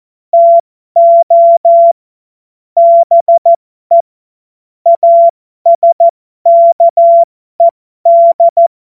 Morse code, heard as a keyed tone, about 14 wpm, spaced as for 12 wpm, 680 Hz. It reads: TO BE ASKED